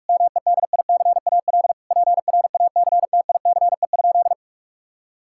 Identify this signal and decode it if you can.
Morse code, heard as a keyed tone, about 41 wpm, 700 Hz. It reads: MEDICAL PRACTICE?
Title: Sentences from Top 1000 Words - Review Part 5 - 45wpm